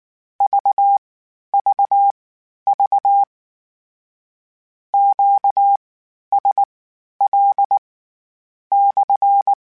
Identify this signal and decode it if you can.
Morse code, heard as a keyed tone, about 19 wpm, spaced as for 10 wpm, 800 Hz. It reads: VVV QSL/
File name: Q Calls / Q Calls-mp3s / Q Calls SPSL1 @18